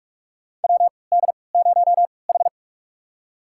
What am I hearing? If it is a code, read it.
Morse code, heard as a keyed tone, about 42 wpm, spaced as for 25 wpm, 700 Hz. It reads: WD0H
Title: USA Call Signs - 4 Characters - 3x Character Spacing - 45wpm